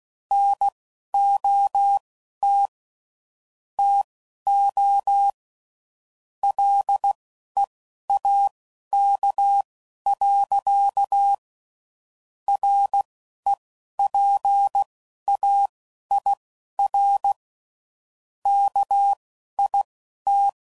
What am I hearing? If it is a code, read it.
Morse code, heard as a keyed tone, about 16 wpm, spaced as for 11 wpm, 800 Hz. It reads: NOT TO LEAK. REPAIR KIT
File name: Sayings / Quotes / Airtight